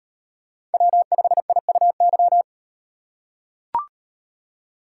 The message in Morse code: W5IUY

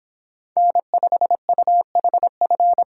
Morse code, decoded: N5UHF

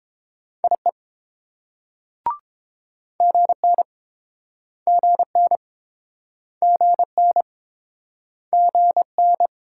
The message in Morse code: IE GN GN GN GN

